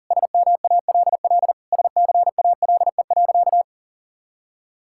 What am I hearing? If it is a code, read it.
Morse code, heard as a keyed tone, about 40 wpm, 700 Hz. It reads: SMALL SCALE.
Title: Sentences from Top 1000 Words - Review Part 5 - 40wpm